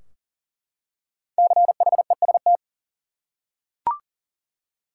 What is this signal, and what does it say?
Morse code, heard as a keyed tone, about 40 wpm, 700 Hz. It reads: CHEST
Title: Five Letter Words - 40wpm